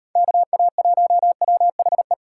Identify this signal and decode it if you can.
Morse code, heard as a keyed tone, about 38 wpm, 700 Hz. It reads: KA1WHE